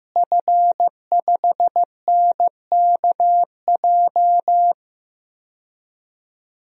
Morse code, heard as a keyed tone, about 15 wpm, 700 Hz. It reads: F5NKJ
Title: DXCC-100 Call Signs - 5 Characters - Repeated 3x - 15wpm